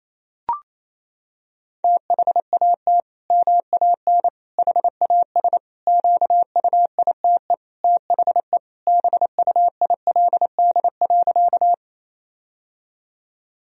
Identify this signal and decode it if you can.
Morse code, heard as a keyed tone, about 27 wpm, 700 Hz. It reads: THAT MAN HAS QUITE THE BUILD.
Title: Sentences from Top 1000 Words - Review Part 1 - 28wpm